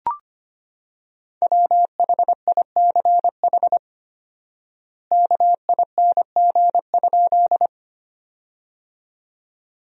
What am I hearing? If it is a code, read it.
Morse code, heard as a keyed tone, about 25 wpm, 700 Hz. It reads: WHICH KING?
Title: 4x Word-Spacing - Sentences from Top 400 Words - 25wpm